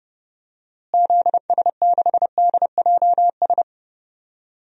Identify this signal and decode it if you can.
Morse code, heard as a keyed tone, about 30 wpm, 700 Hz. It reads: ZS6DJS
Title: DXCC-100 Call Signs - 6 Characters - Repeated 2x - 30wpm